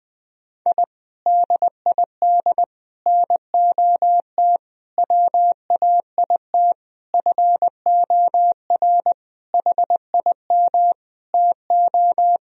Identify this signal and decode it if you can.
Morse code, heard as a keyed tone, about 20 wpm, 700 Hz. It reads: I DID NOT WAIT FOR HIM TO
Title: Sentences from Top 1000 Words - Review Part 2 - 20wpm